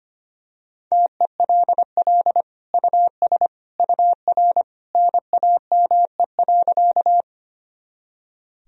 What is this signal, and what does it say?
Morse code, heard as a keyed tone, about 25 wpm, 700 Hz. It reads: TELL US UR NAME.
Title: Sentences from Top 200 Words and Abbreviations 25wpm